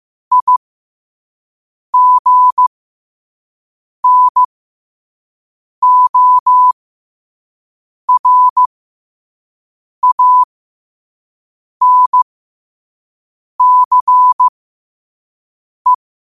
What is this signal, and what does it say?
Morse code, heard as a keyed tone, about 15 wpm, spaced as for 5 wpm, 1000 Hz. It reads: IGNORANCE
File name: qotdfiles / 5wpm